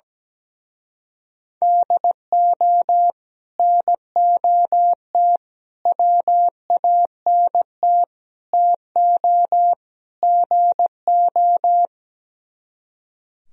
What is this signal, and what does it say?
Morse code, heard as a keyed tone, about 17 wpm, 700 Hz. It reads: DO NOT WANT TO GO
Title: Speed-Racing - Sets of 5 Words 17wpm